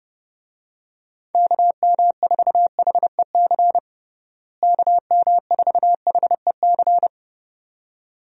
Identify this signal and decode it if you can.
Morse code, heard as a keyed tone, about 32 wpm, 700 Hz. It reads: KM4HEC KM4HEC
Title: USA Call Signs - 6 Characters - Repeated 2x - 30wpm